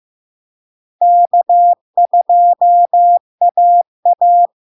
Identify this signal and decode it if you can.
Morse code, heard as a keyed tone, about 15 wpm, 700 Hz. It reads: K2AA
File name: LARK-FD-2022-Log-in-CW-15wpm-700-Hz